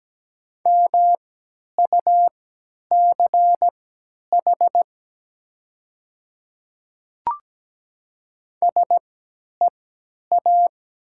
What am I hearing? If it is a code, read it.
Morse code, heard as a keyed tone, about 17 wpm, spaced as for 10 wpm, 700 Hz. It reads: MUCH SEA
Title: 3x Character Spacing - Top 300 Words - 17wpm